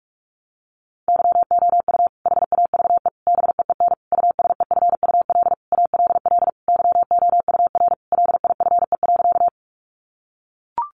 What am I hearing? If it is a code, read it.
Morse code, heard as a keyed tone, about 44 wpm, 700 Hz. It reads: YOU HAVE BEEN USEFUL ALL YOUR LIFE.
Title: Sentences from Top 700 Words 45wpm